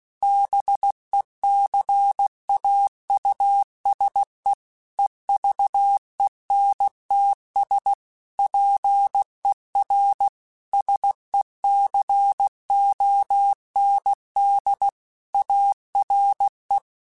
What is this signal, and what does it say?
Morse code, heard as a keyed tone, about 16 wpm, 800 Hz. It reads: BECAUSE EVENTS PER SECOND ARE
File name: Technical Subjects / Technical Subjects-mp3s / XL @15